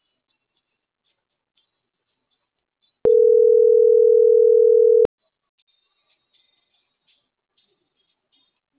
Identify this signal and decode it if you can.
Ringback tone (North America), 440 + 480 Hz.